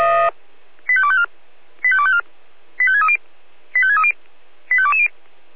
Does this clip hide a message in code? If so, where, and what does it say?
Yes: five-tone selective call (ZVEI) from 0.9 to 5.1 s, heard as quick runs of five tones, tones of about 70 ms. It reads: 85254 85254 85630 85630 85220